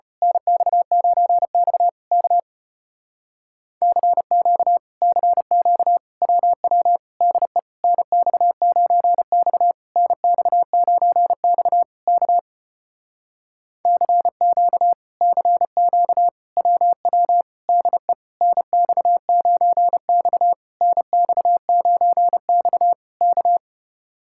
Morse code, heard as a keyed tone, about 34 wpm, 700 Hz. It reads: NX9X K CQ CQ WW DE NX9X NX9X K CQ CQ WW DE NX9X NX9X K